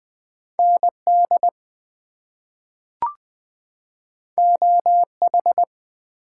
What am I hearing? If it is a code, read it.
Morse code, heard as a keyed tone, about 20 wpm, 700 Hz. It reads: ND OH
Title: US States and Canadian Provinces 20wpm